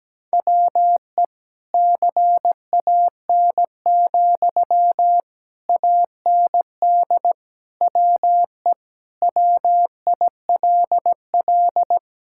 Morse code, heard as a keyed tone, about 17 wpm, 700 Hz. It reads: WE CAN, AND WE WILL
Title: Sentences from Top 1000 Words - Review Part 6 - 17wpm